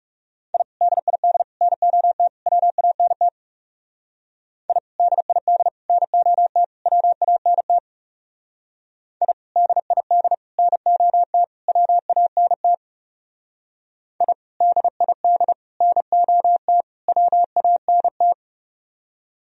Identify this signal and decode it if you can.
Morse code, heard as a keyed tone, about 36 wpm, 700 Hz. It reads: I DID NOT WANT I DID NOT WANT I DID NOT WANT I DID NOT WANT
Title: Speed-Racing - Sets of 4 Words 30wpm